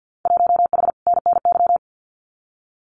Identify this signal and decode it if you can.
Morse code, heard as a keyed tone, about 50 wpm, 700 Hz. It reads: 1H NNY
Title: Winter Field Day Exchanges 50wpm